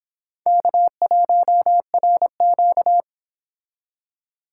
Morse code, heard as a keyed tone, about 30 wpm, 700 Hz. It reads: K1RQ